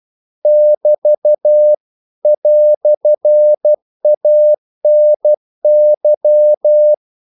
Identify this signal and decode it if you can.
Morse code, heard as a keyed tone, about 12 wpm, 600 Hz. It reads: <BT> "ANY
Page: English quotations in CW (telegraphy - morse)